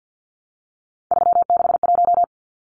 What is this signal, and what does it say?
Morse code, heard as a keyed tone, about 50 wpm, 700 Hz. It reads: 361
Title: Numbers - 3-Digit 50wpm